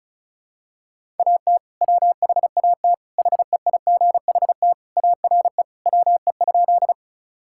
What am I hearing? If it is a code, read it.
Morse code, heard as a keyed tone, about 34 wpm, 700 Hz. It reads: AT WHAT HEIGHT ARE WE?
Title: Sentences from Top 1600 Words 35wpm